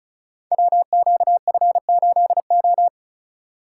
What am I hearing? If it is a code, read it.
Morse code, heard as a keyed tone, about 36 wpm, 700 Hz. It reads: WQF8O
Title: Code Groups - 5 Letter:Numbers - 35wpm